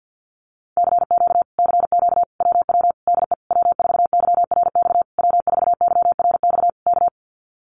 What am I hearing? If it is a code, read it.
Morse code, heard as a keyed tone, about 50 wpm, 700 Hz. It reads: CQ CQ WW DE W4YRX W4YRX K
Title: Common QSO Phrases Encore 50wpm